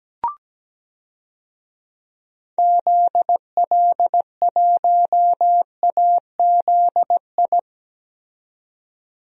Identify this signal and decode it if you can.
Morse code, heard as a keyed tone, about 17 wpm, 700 Hz. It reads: ZL1AZI
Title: DXCC-100 Call Signs - 6 Characters - 17wpm